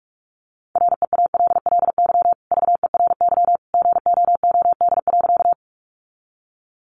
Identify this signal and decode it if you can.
Morse code, heard as a keyed tone, about 44 wpm, 700 Hz. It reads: REALLY VERY GOOD.
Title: Sentences from Top 600 Words 45wpm